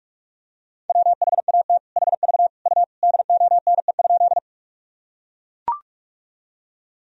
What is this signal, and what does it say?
Morse code, heard as a keyed tone, about 43 wpm, 700 Hz. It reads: WHAT HV U DONE?
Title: Sentences from Top 500 Words and Abbreviations 45wpm